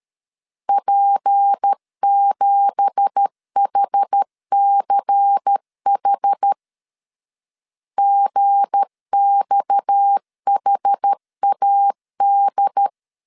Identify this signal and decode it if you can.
Morse code, heard as a keyed tone, about 13 wpm, 800 Hz. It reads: P7HCH GXHAD
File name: LEZIONI / 10 SIGNS / 10C DOT 12wpm 2451